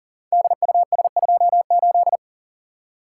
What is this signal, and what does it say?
Morse code, heard as a keyed tone, about 40 wpm, 700 Hz. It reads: DUS28
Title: Code Groups - 5 Letter:Numbers - 40wpm